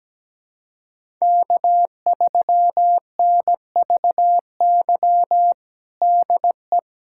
Morse code, heard as a keyed tone, about 17 wpm, 700 Hz. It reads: K3NVY DE
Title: USA Callsign DE Callsign - 5 Characters - 17wpm